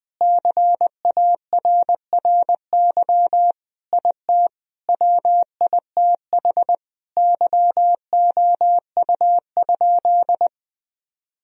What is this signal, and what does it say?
Morse code, heard as a keyed tone, about 21 wpm, 700 Hz. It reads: CARRY IT WITH YOU?
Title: Sentences from Top 1000 Words - Review Part 1 - 20wpm